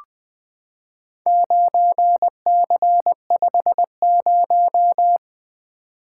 Morse code, heard as a keyed tone, about 20 wpm, 700 Hz. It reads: 9C50
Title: Four Letter-Number Combinations - 20wpm